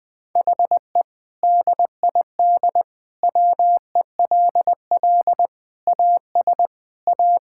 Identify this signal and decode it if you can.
Morse code, heard as a keyed tone, about 20 wpm, 700 Hz. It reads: HE DID WELL AS A
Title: Sentences with 2-Word Phrases - Spoken Beginning - 20wpm